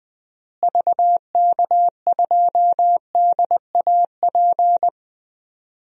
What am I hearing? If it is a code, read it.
Morse code, heard as a keyed tone, about 20 wpm, 700 Hz. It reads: VK2DAP
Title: DXCC-100 Call Signs - 6 Characters - Repeated 3x - 20wpm